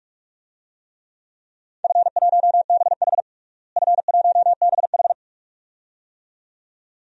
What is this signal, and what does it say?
Morse code, heard as a keyed tone, about 41 wpm, 700 Hz. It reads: F1BH F1BH
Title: DXCC-100 Call Signs - 4 Characters - Repeated 2x - 45wpm